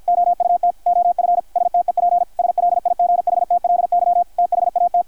Morse code, heard as a keyed tone, about 48 wpm, 660 Hz, with radio noise. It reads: OUT OF STEP SLIGHTLY THAT
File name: archived / CRSnet